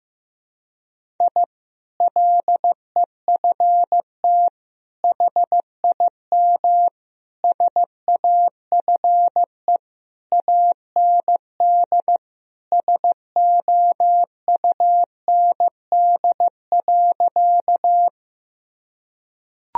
Morse code, heard as a keyed tone, about 15 wpm, 700 Hz. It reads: I LEFT HIM SAFE AND SOUND.
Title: Sentences from Top 1000 Words - Review Part 1 - 15wpm